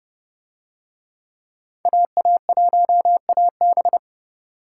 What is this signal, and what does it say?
Morse code, heard as a keyed tone, about 29 wpm, 700 Hz. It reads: AA1AB